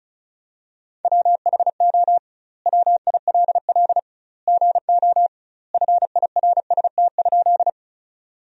Morse code, heard as a keyed tone, about 36 wpm, 700 Hz. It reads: WHO WILL GO FIRST?
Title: 2x Word-Spacing - Sentences from Top 100 Words - 35wpm